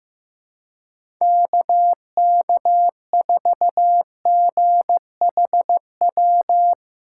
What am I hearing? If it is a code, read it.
Morse code, heard as a keyed tone, about 15 wpm, 700 Hz. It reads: KK4GHW
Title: USA Call Signs - 6 Characters - 4x Character Spacing - 15wpm